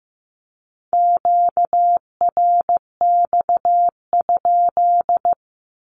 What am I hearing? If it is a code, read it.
Morse code, heard as a keyed tone, about 15 wpm, 700 Hz. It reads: QRX?
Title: Q-Codes 15wpm